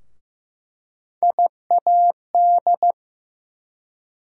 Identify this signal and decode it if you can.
Morse code, heard as a keyed tone, about 15 wpm, 700 Hz. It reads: IAD
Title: USA Call Sign Postfixes - 3 Letters 15wpm